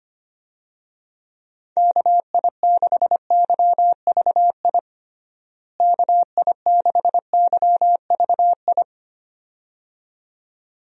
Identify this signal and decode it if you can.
Morse code, heard as a keyed tone, about 25 wpm, 700 Hz. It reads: KI6YVI KI6YVI
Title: USA Call Signs - 6 Characters - Repeated 2x - 25wpm